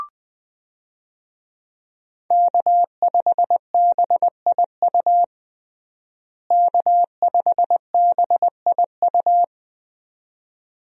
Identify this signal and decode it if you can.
Morse code, heard as a keyed tone, about 21 wpm, 700 Hz. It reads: K5BIU K5BIU
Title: USA Call Signs - 5 Characters - Repeated 2x - 20wpm